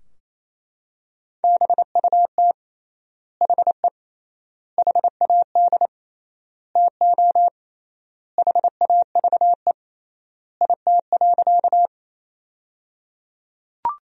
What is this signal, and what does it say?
Morse code, heard as a keyed tone, about 28 wpm, 700 Hz. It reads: BUT HE HAD TO HAVE IT.
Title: 3x Word-Spacing - Sentences from Top 100 Words - 28wpm